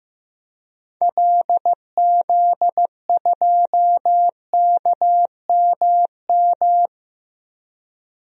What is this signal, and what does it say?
Morse code, heard as a keyed tone, about 15 wpm, 700 Hz. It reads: LZ2KMM
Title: DXCC-100 Call Signs - 6 Characters - Repeated 2x - 15wpm